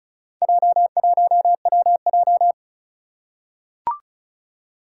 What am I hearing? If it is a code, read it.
Morse code, heard as a keyed tone, about 35 wpm, 700 Hz. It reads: J1WJ
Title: Four Challenging Letter-Number Combinations 35wpm